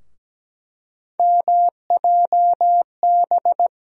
Morse code, heard as a keyed tone, about 17 wpm, 700 Hz. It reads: MJB
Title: USA Call Sign Postfixes - 3 Letters 17wpm